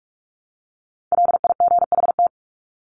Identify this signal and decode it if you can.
Morse code, heard as a keyed tone, about 44 wpm, 700 Hz. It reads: LIGHT
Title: Top 300 Words 45wpm